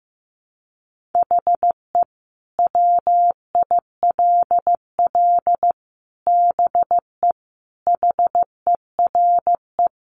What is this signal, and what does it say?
Morse code, heard as a keyed tone, about 15 wpm, 700 Hz. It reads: HE WILL BE HERE
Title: Sentences from Top 600 Words 15wpm